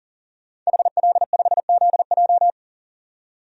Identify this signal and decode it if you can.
Morse code, heard as a keyed tone, about 40 wpm, 700 Hz. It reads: HL5ZJ
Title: DXCC-100 Call Signs - 5 Characters - 40wpm